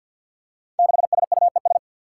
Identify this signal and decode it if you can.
Morse code, heard as a keyed tone, about 47 wpm, 700 Hz. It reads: 6SUES
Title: Five Challenging Letter-Number Combinations 50wpm